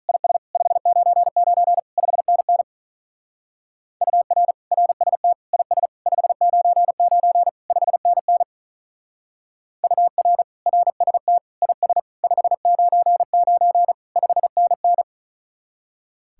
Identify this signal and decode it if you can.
Morse code, heard as a keyed tone, about 46 wpm, 700 Hz. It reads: IS 599 5NN UR RST IS 599 5NN UR RST IS 599 5NN